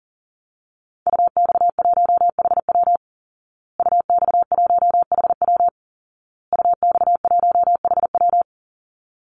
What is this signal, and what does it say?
Morse code, heard as a keyed tone, about 40 wpm, 700 Hz. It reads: UX1HW UX1HW UX1HW
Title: Call Signs Repeated 2x - 40wpm